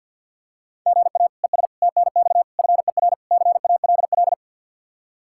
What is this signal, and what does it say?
Morse code, heard as a keyed tone, about 51 wpm, 700 Hz. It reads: GA ES TNX FER CALL
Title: Practice QSO - Exchange 1 - Beginning - 50wpm